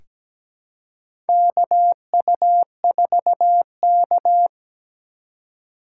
Morse code, heard as a keyed tone, about 17 wpm, 700 Hz. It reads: KU4K
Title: USA Call Signs - 4 Characters - 3x Character Spacing - 17wpm